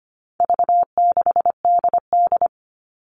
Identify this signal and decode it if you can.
Morse code, heard as a keyed tone, about 25 wpm, 700 Hz. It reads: V6DD
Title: All Cut Numbers - 4-Digit 25wpm